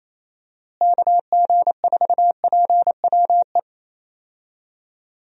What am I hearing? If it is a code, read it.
Morse code, heard as a keyed tone, about 28 wpm, 700 Hz. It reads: KG4PWE
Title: USA Call Signs - 6 Characters - Repeated 3x - 28wpm